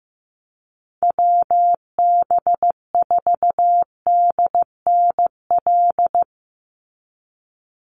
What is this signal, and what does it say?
Morse code, heard as a keyed tone, about 15 wpm, 700 Hz. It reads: WB4DNL
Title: Call Signs Repeated 2x - 15wpm